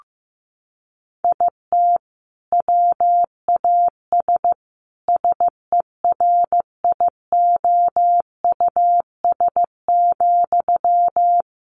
Morse code, heard as a keyed tone, about 15 wpm, 700 Hz. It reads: IT WAS SERIOUS,
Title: Sentences from Top 1100 Words 15wpm